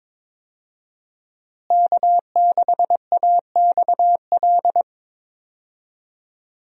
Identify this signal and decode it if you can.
Morse code, heard as a keyed tone, about 21 wpm, 700 Hz. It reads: K6AXL
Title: USA Call Signs - 5 Characters - Repeated 3x - 22wpm